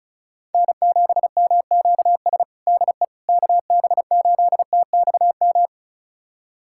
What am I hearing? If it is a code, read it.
Morse code, heard as a keyed tone, about 36 wpm, 700 Hz. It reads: N7MQS DE KB8TXM
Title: USA Callsign DE Callsign 35wpm